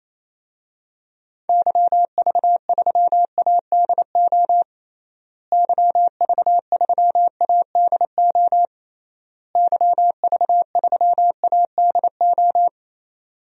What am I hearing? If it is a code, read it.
Morse code, heard as a keyed tone, about 28 wpm, 700 Hz. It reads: YV3ADO YV3ADO YV3ADO
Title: DXCC-100 Call Signs - 6 Characters - Repeated 3x - 28wpm